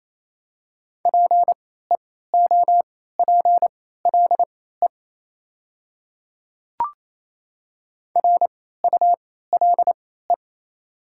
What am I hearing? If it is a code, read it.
Morse code, heard as a keyed tone, about 27 wpm, spaced as for 16 wpm, 700 Hz. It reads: PEOPLE RULE
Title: 3x Character Spacing - Top 500 Words - 28wpm